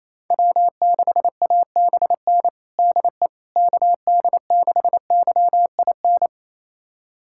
Morse code, heard as a keyed tone, about 28 wpm, 700 Hz. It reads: W6ABN DE KD6YIN
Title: USA Callsign DE Callsign 28wpm